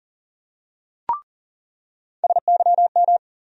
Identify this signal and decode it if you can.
Morse code, heard as a keyed tone, about 40 wpm, 700 Hz. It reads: SYM
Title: USA Call Sign Postfixes - 3 Letters 40wpm